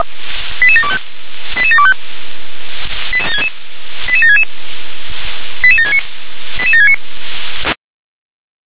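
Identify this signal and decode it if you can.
Five-tone selective call (ZVEI), heard as quick runs of five tones, tones of about 70 ms. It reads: 88725 88725 88766 88769 88769
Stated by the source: Five-tone ZVEI selective call received on SDR.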